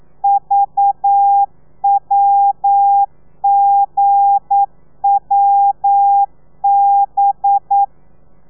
Morse code, heard as a keyed tone, about 9 wpm, 800 Hz. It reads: VWGWB
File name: telegrafia / i2tfj / Lez8 40